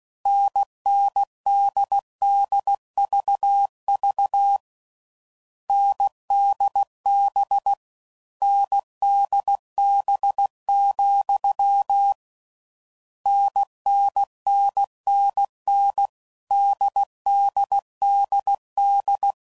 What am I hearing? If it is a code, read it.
Morse code, heard as a keyed tone, about 16 wpm, 800 Hz. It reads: NNDDVV NDB NDB, NNNNN DDDD
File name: Combinations / Difficult Combos / NDB NDB6